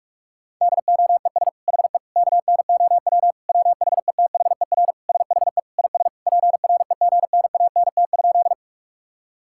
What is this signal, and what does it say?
Morse code, heard as a keyed tone, about 42 wpm, 700 Hz. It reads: DOES HE KNOW WHETHER SHE IS PREGNANT?
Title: Sentences from Top 1000 Words - Review Part 6 - 45wpm